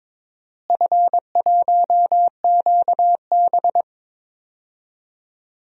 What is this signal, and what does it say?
Morse code, heard as a keyed tone, about 21 wpm, 700 Hz. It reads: F1QB
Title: DXCC-100 Call Signs - 4 Characters - Repeated 2x - 22wpm